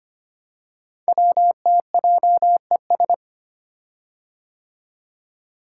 Morse code, heard as a keyed tone, about 25 wpm, 700 Hz. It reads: WTJES
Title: Code Groups - 5 Letters - 25wpm